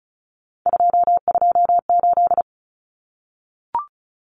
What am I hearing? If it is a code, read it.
Morse code, heard as a keyed tone, about 34 wpm, 700 Hz. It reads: 228
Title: Common Cut Numbers - 3-Digit 35wpm